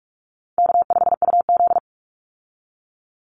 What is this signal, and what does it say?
Morse code, heard as a keyed tone, about 44 wpm, 700 Hz. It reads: K5UZ
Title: Easy Call Signs Repeated 3x - 45wpm